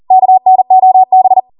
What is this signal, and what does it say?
Morse code, heard as a keyed tone, about 40 wpm, 750 Hz. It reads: KNOB
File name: Morse / 40WPM